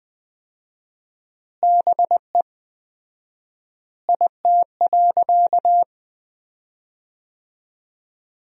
Morse code, heard as a keyed tone, about 20 wpm, 700 Hz. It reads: BE IT.
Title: 4x Word-Spacing - Sentences from Top 100 Words - 20wpm